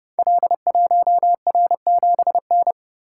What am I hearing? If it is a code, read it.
Morse code, heard as a keyed tone, about 30 wpm, 700 Hz. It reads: L1R7N
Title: Code Groups - 5 Letter:Numbers - 30wpm